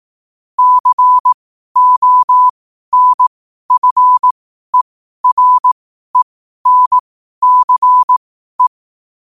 Morse code, heard as a keyed tone, about 18 wpm, spaced as for 12 wpm, 1000 Hz. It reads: CONFERENCE
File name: qotdfiles / 12wpm